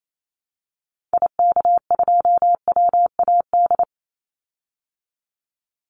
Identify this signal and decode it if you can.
Morse code, heard as a keyed tone, about 28 wpm, 700 Hz. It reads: IK2WAD
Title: Call Signs Repeated 2x - 28wpm